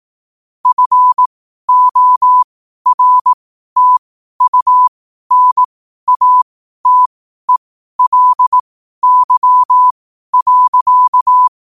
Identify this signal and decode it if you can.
Morse code, heard as a keyed tone, about 18 wpm, spaced as for 12 wpm, 1000 Hz. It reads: FORTUNATELY.
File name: qotdfiles / 12wpm